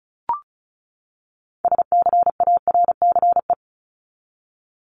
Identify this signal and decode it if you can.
Morse code, heard as a keyed tone, about 34 wpm, 700 Hz. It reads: SCARCE
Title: LY Suffix - Top 100 Words with Base Words 35wpm